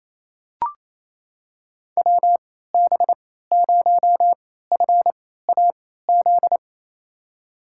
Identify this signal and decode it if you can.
Morse code, heard as a keyed tone, about 28 wpm, spaced as for 16 wpm, 700 Hz. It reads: WB0FAZ
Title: USA Call Signs - 3x Character Spacing - 28wpm